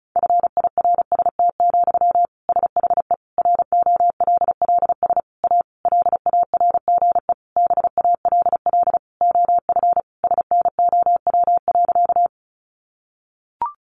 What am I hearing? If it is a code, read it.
Morse code, heard as a keyed tone, about 34 wpm, 700 Hz. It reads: FIRST, SHE ROLLS A LARGE BALL OF SNOW.
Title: Easy Phrases 35wpm